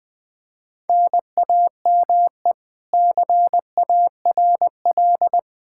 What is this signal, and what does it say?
Morse code, heard as a keyed tone, about 21 wpm, 700 Hz. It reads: NAME CARL
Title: Practice QSO - Exchange 1 - Beginning - 20wpm